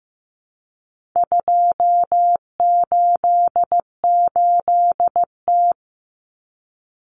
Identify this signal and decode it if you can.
Morse code, heard as a keyed tone, about 15 wpm, 700 Hz. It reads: 288T
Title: Common Cut Numbers - 4-Digit 15wpm